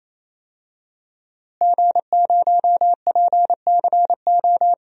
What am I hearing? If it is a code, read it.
Morse code, heard as a keyed tone, about 28 wpm, 700 Hz. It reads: G0PCO
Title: DXCC-100 Call Signs - 5 Characters - Repeated 3x - 28wpm